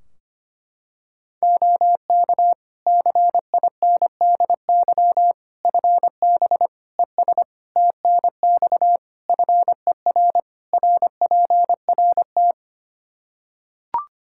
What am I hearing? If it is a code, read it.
Morse code, heard as a keyed tone, about 25 wpm, 700 Hz. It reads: OK CINDY FB ES TNX FER RPRT
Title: K1USN SST QSO Practice 25wpm